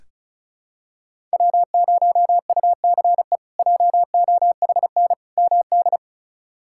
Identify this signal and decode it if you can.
Morse code, heard as a keyed tone, about 34 wpm, 700 Hz. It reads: W0UCE JOHN MD
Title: K1USN SST QSO Practice - 35wpm